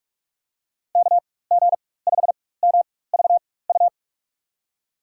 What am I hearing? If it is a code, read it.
Morse code, heard as a keyed tone, about 46 wpm, spaced as for 21 wpm, 700 Hz. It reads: KG5MVU
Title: USA Call Signs - 6 Characters - 4x Character Spacing - 45wpm